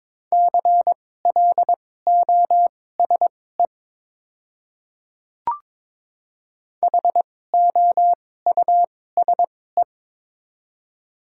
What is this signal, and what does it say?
Morse code, heard as a keyed tone, about 22 wpm, spaced as for 16 wpm, 700 Hz. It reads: CLOSE HOUSE
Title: 2x Character Spacing - Top 300 Words - 22wpm